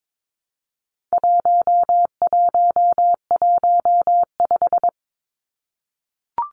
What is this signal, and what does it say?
Morse code, heard as a keyed tone, about 22 wpm, 700 Hz. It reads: 1115
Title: Numbers - 4-Digit 22wpm